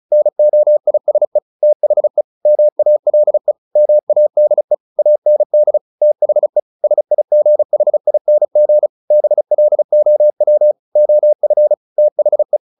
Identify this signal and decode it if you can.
Morse code, heard as a keyed tone, about 34 wpm, 600 Hz. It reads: NOISE THE MALE MADE AND THE SIGHING BLOW OF THE
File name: the old man and the sea / the old man and the sea 1 35wpm